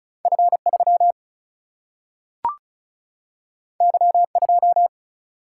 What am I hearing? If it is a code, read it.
Morse code, heard as a keyed tone, about 36 wpm, 700 Hz. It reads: F 3 Y 2
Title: Single-Letter Single-Number 35wpm